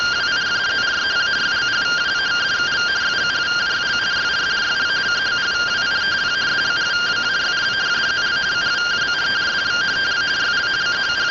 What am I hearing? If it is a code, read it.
Two-tone teleprinter signal, about 170 Hz shift.